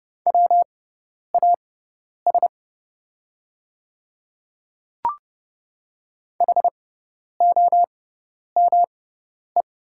Morse code, heard as a keyed tone, about 30 wpm, spaced as for 10 wpm, 700 Hz. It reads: WAS HOME
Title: 6x Character Spacing - Top 300 Words - 30wpm